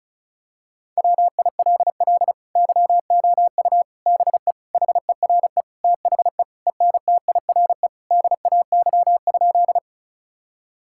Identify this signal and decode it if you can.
Morse code, heard as a keyed tone, about 36 wpm, 700 Hz. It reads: WILL YOU BE HERE THE ENTIRE DAY?
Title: Sentences from Top 1000 Words - Review Part 5 - 35wpm